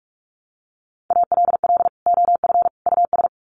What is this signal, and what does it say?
Morse code, heard as a keyed tone, about 44 wpm, 700 Hz. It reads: ALL OF US
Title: Sets of 3 Words 45wpm